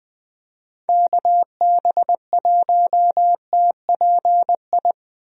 Morse code, heard as a keyed tone, about 21 wpm, 700 Hz. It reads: KB1TPI
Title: USA Call Signs - 6 Characters - Repeated 3x - 20wpm